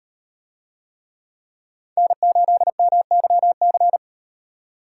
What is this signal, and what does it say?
Morse code, heard as a keyed tone, about 36 wpm, 700 Hz. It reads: N8MYC